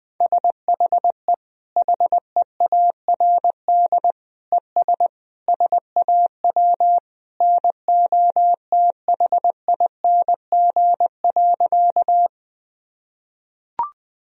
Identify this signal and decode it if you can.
Morse code, heard as a keyed tone, about 20 wpm, 700 Hz. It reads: SHE HEARD ES SAW NOTHING.
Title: Sentences from Top 500 Words and Abbreviations 20wpm